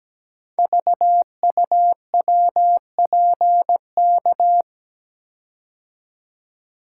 Morse code, heard as a keyed tone, about 17 wpm, 700 Hz. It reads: VUWPK
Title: Code Groups - 5 Letters - 17wpm